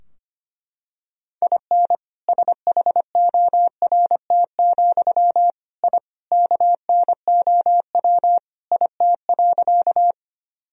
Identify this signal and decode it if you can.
Morse code, heard as a keyed tone, about 25 wpm, 700 Hz. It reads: IN SHORT, I KNOW IT.
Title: Sentences from Top 1000 Words - Review Part 2 - 25wpm